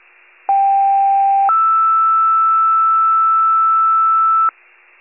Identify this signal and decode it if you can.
Two-tone page, heard as a one-second tone, then a three-second tone, about 790 then 1320 Hz. It is heard from 0.5 to 4.5 s.
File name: wcalert